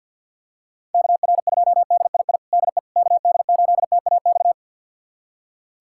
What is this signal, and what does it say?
Morse code, heard as a keyed tone, about 50 wpm, 700 Hz. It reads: KR2DII DE KD8TAX
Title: USA Callsign DE Callsign 50wpm